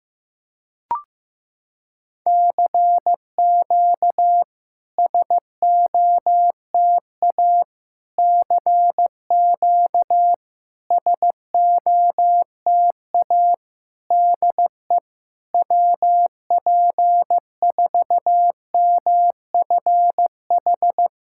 Morse code, heard as a keyed tone, about 16 wpm, 700 Hz. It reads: CQ SOTA CQ SOTA DE WP4MFH
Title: SOTA QSO Practice 15wpm